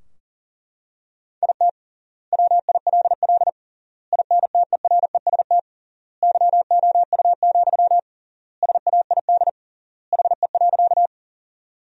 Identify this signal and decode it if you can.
Morse code, heard as a keyed tone, about 37 wpm, 700 Hz. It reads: IT WILL INTEREST YOU, SAID HE.
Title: 3x Word-Spacing - Sentences from Top 400 Words - 40wpm